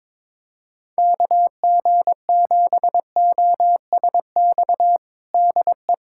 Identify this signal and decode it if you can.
Morse code, heard as a keyed tone, about 21 wpm, 700 Hz. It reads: KG7OSX DE
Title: USA Callsign DE Callsign 22wpm